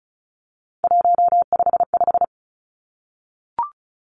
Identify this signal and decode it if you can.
Morse code, heard as a keyed tone, about 36 wpm, 700 Hz. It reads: 155
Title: Three Challenging Letter-Number Combinations 35wpm